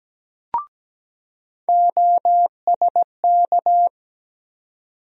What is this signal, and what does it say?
Morse code, heard as a keyed tone, about 17 wpm, 700 Hz. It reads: OSK
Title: USA Call Sign Postfixes - 3 Letters 17wpm